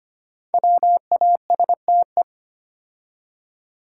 Morse code, heard as a keyed tone, about 25 wpm, 700 Hz. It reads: WASTE
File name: 5 Letter Words - First 3 Letters Given - 25wpm